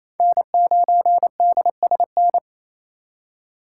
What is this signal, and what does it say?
Morse code, heard as a keyed tone, about 28 wpm, 700 Hz. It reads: N9DSN
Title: USA Call Signs - 3x Character Spacing - 28wpm